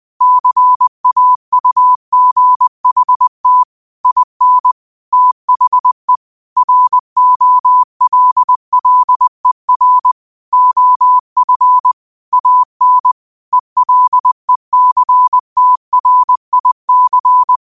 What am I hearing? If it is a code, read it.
Morse code, heard as a keyed tone, about 20 wpm, 1000 Hz. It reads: CAUGHT IN THE ROLLER OF AN ELECTRIC
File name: qotdfiles / 20wpm